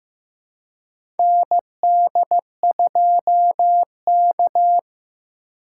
Morse code, heard as a keyed tone, about 15 wpm, 690 Hz. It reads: ND2K
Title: USA Call Signs - 4 Characters - 2x Character Spacing - 15wpm